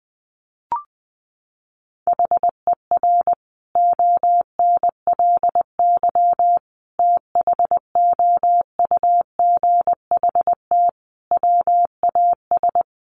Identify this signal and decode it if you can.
Morse code, heard as a keyed tone, about 20 wpm, 700 Hz. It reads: HER ONLY THOUGHT WAS
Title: Sentences from Top 400 Words 20wpm